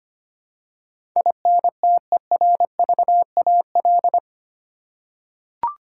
Morse code, heard as a keyed tone, about 25 wpm, 700 Hz. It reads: INTERVAL
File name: 8 Letter Words - First 3 Letters Given - 25wpm